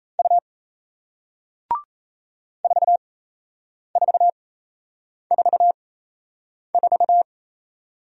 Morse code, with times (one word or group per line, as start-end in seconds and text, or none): 0.2-0.4 U
2.6-7.2 4444